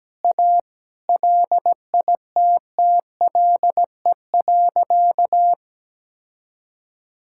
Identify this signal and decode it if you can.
Morse code, heard as a keyed tone, about 17 wpm, 700 Hz. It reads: A LITTLE.
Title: Sentences from Top 1000 Words - Review Part 6 - 17wpm